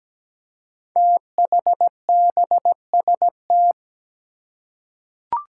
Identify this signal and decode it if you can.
Morse code, heard as a keyed tone, about 17 wpm, 700 Hz. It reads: THBST
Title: Code Groups - 5 Letters - 17wpm